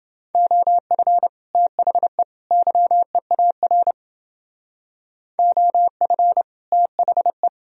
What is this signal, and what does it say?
Morse code, heard as a keyed tone, about 30 wpm, 700 Hz. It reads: OF THE YEAR OF THE